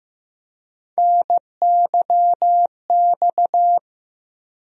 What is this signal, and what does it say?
Morse code, heard as a keyed tone, about 15 wpm, 700 Hz. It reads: NYX
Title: USA Call Sign Postfixes - 3 Letters 15wpm